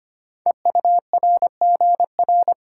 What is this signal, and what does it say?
Morse code, heard as a keyed tone, about 25 wpm, 700 Hz. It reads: EURGR
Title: Code Groups - 5 Letters - 25wpm